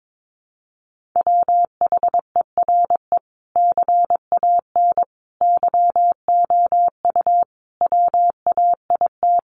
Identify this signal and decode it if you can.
Morse code, heard as a keyed tone, about 22 wpm, 700 Hz. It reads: WHERE CAN YOU WAIT
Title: Sentences from Top 500 Words 22wpm